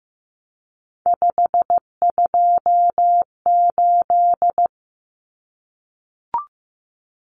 Morse code, heard as a keyed tone, about 15 wpm, 700 Hz. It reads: 528
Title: Numbers - 3-Digit 15wpm